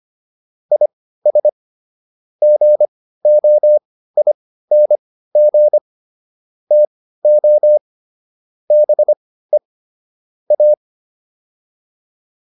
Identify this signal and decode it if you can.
Morse code, heard as a keyed tone, about 25 wpm, spaced as for 15 wpm, 600 Hz. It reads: IS GOING TO BE A